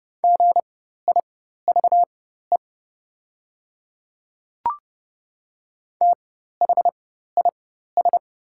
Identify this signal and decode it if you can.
Morse code, heard as a keyed tone, about 30 wpm, spaced as for 14 wpm, 700 Hz. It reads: GIVE THIS
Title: 4x Character Spacing - Top 200 Words - 30wpm